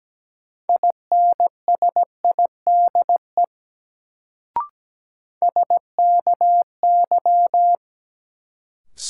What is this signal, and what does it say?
Morse code, heard as a keyed tone, about 17 wpm, 700 Hz. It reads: INSIDE SKY
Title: Sentences from Top 1200 Words - New Word Drill Alt - 17wpm